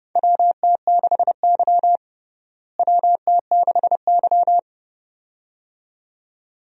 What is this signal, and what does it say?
Morse code, heard as a keyed tone, about 30 wpm, 700 Hz. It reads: WT6Y WT6Y
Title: USA Call Signs - 4 Characters - Repeated 2x - 30wpm